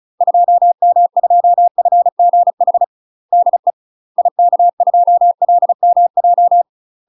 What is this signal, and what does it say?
Morse code, heard as a keyed tone, about 34 wpm, 700 Hz. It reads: 2M2FGH DE IK2LMJ
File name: UK MASTER QSOs TRAINING / 35WPM / UK MASTER QSOs 0004